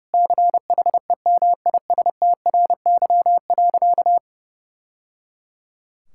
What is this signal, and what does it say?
Morse code, heard as a keyed tone, about 30 wpm, 700 Hz. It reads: CHEMISTRY.
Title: Sentences from Top 1000 Words - Review Part 6 - 30wpm